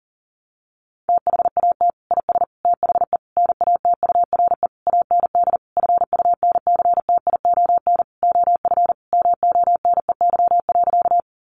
Morse code, heard as a keyed tone, about 40 wpm, 700 Hz. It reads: THAT IS THE NATURE AND FUNCTION OF MONEY.
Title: Sentences from Top 1100 Words 40wpm